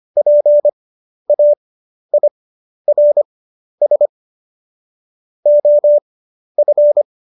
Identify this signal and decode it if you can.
Morse code, heard as a keyed tone, about 25 wpm, spaced as for 11 wpm, 590 Hz. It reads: PAIRS OF
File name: Jokes-25-char-10-eff